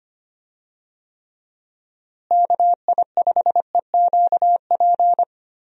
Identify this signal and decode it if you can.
Morse code, heard as a keyed tone, about 25 wpm, 700 Hz. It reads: KI5EQP
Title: USA Call Signs - 6 Characters - Repeated 3x - 25wpm